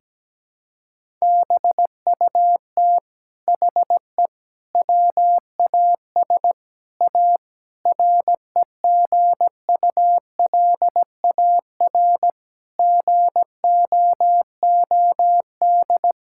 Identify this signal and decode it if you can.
Morse code, heard as a keyed tone, about 17 wpm, 700 Hz. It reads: BUT HE WAS A REGULAR GOOD
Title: Sentences from Top 1000 Words - Review Part 6 - 17wpm